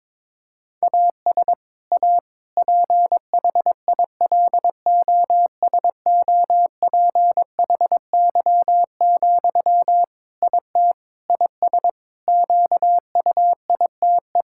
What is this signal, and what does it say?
Morse code, heard as a keyed tone, about 22 wpm, 700 Hz. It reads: AS A PHILOSOPHY, IT IS QUITE
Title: Sentences from Top 1000 Words - Review Part 6 - 22wpm